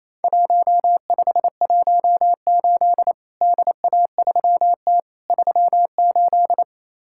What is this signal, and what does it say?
Morse code, heard as a keyed tone, about 28 wpm, 700 Hz. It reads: 1518 DA3T 38
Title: Worked All Europe DX Contest - QTC Exchange - 28wpm